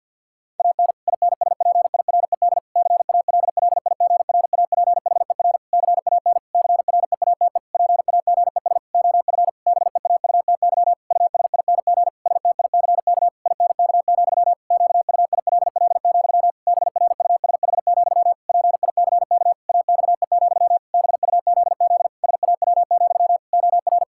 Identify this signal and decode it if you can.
Morse code, heard as a keyed tone, about 50 wpm, 700 Hz. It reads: AN INSPIRED CALLIGRAPHER CAN CREATE PAGES OF BEAUTY USING STICK INK, QUILL, BRUSH, PICK AXE, BUZZ SAW, OR
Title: Pangram Challenge v2 - 50wpm